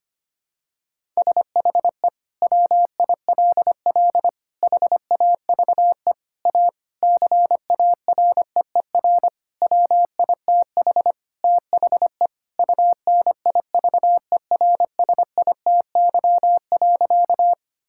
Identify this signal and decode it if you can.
Morse code, heard as a keyed tone, about 25 wpm, 700 Hz. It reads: SHE WILL HAVE A CAREER WITH THE UNIVERSITY.
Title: Sentences from Top 1000 Words - Review Part 6 - 25wpm